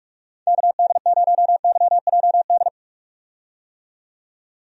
Morse code, heard as a keyed tone, about 43 wpm, 700 Hz. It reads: KD0YJD
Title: USA Call Signs - 6 Characters - 45wpm